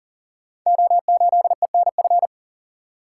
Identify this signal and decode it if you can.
Morse code, heard as a keyed tone, about 40 wpm, 690 Hz. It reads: O8ENF
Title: Code Groups - 5 Letter:Numbers - 40wpm